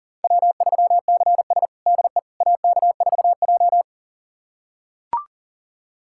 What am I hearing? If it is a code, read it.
Morse code, heard as a keyed tone, about 40 wpm, 700 Hz. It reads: W3CS DE AK4J
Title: USA Callsign DE Callsign - 4 Characters - 40wpm